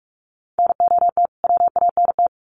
Morse code, heard as a keyed tone, about 44 wpm, 700 Hz. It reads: NOT WANT
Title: Sets of 2 Words 45wpm